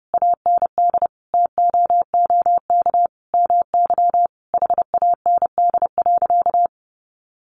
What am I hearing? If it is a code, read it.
Morse code, heard as a keyed tone, about 30 wpm, 700 Hz. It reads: AND TOOK MY HAND.
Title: Sentences from Top 1100 Words 30wpm